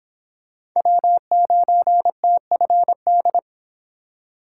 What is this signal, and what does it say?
Morse code, heard as a keyed tone, about 26 wpm, 700 Hz. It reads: W9TFD